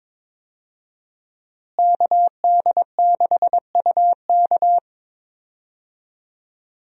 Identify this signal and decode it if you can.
Morse code, heard as a keyed tone, about 22 wpm, 700 Hz. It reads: KD6UK
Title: USA Call Signs - 5 Characters - Repeated 2x - 22wpm